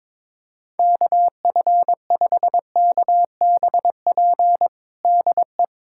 Morse code, heard as a keyed tone, about 22 wpm, 700 Hz. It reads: KF5KBP DE
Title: USA Callsign DE Callsign 22wpm